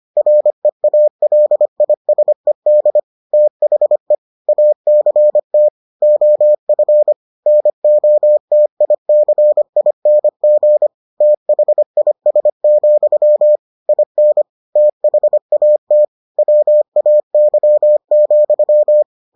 Morse code, heard as a keyed tone, about 25 wpm, 600 Hz. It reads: REALISED THE ACT OF NOTICING THIS, IN THAT WAY,